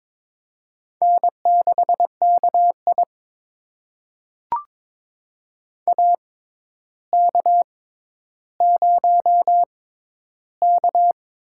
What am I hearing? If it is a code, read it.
Morse code, heard as a keyed tone, about 22 wpm, spaced as for 11 wpm, 700 Hz. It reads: N6KI AK0K
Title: USA Call Signs - 4 Characters - 6x Character Spacing - 22wpm